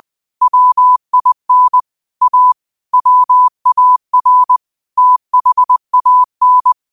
Morse code, heard as a keyed tone, about 20 wpm, 1000 Hz. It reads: WIN A WAR THAN